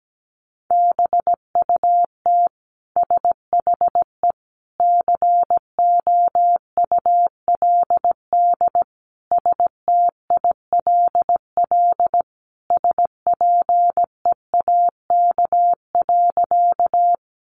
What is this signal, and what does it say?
Morse code, heard as a keyed tone, about 17 wpm, 700 Hz. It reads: BUT SHE COULD STILL SPEAK.
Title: Sentences from Top 1100 Words 17wpm